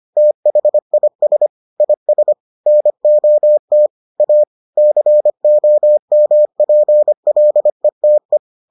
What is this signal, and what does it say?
Morse code, heard as a keyed tone, about 25 wpm, 600 Hz. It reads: THIS IS NOT A COMPLETE